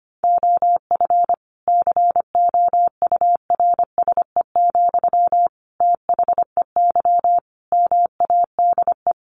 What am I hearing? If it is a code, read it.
Morse code, heard as a keyed tone, about 25 wpm, 700 Hz. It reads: OF COURSE, THEY MADE
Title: Sentences from Top 1400 Words 25wpm